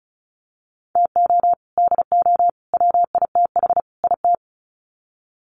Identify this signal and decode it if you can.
Morse code, heard as a keyed tone, about 34 wpm, 700 Hz. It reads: TO DO WITH IT
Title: Sets of 4 Words 35wpm